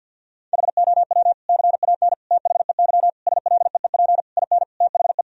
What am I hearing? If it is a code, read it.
Morse code, heard as a keyed tone, about 48 wpm, 700 Hz. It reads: HOW CAN THEY SLEEP IN THE
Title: Sentences from Top 1000 Words - Review Part 3 - 50wpm